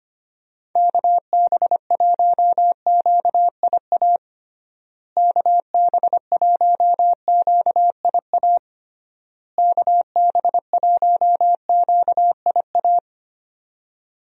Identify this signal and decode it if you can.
Morse code, heard as a keyed tone, about 25 wpm, 700 Hz. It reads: KB1QIA KB1QIA KB1QIA
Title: USA Call Signs - 6 Characters - Repeated 3x - 25wpm